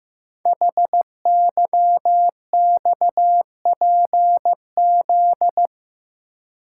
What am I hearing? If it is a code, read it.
Morse code, heard as a keyed tone, about 15 wpm, 700 Hz. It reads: HYXPZ
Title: Code Groups - 5 Letters - 15wpm